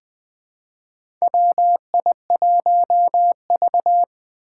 Morse code, heard as a keyed tone, about 20 wpm, 700 Hz. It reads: WI1V
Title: USA Call Signs - 4 Characters - 3x Character Spacing - 20wpm